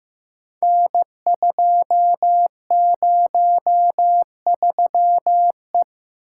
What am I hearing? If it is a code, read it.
Morse code, heard as a keyed tone, about 15 wpm, 700 Hz. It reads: N203E
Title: Code Groups - 5 Letter:Numbers - 15wpm